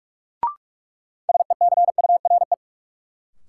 Morse code, heard as a keyed tone, about 46 wpm, 700 Hz. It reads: SECURE
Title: Sentences from Top 1200 Words - New Word Drill Alt - 45wpm